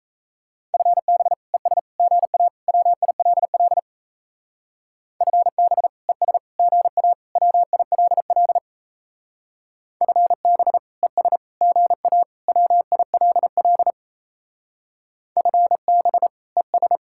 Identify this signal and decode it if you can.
Morse code, heard as a keyed tone, about 32 wpm, 700 Hz. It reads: FB ES GA WILL FB ES GA WILL FB ES GA WILL FB ES
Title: Speed-Racing Common QSO Phrases Encore - 28wpm